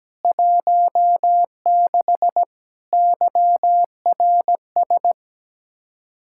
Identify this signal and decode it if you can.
Morse code, heard as a keyed tone, about 17 wpm, 700 Hz. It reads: 16 YRS
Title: Practice QSO - Exchange 3 - More Info - 17wpm